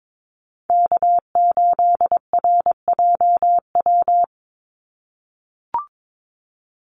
Morse code, heard as a keyed tone, about 22 wpm, 700 Hz. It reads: K8RJW
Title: Call Signs - 22wpm